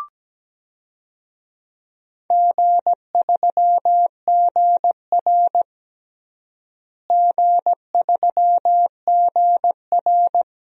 Morse code, heard as a keyed tone, about 17 wpm, 700 Hz. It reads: G3GR G3GR
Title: DXCC-100 Call Signs - 4 Characters - Repeated 2x - 17wpm